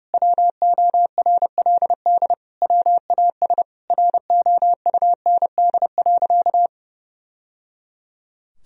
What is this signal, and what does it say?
Morse code, heard as a keyed tone, about 30 wpm, 700 Hz. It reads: WORLD WAS ROUND.
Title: Sentences from Top 1000 Words - Review Part 1 - 30wpm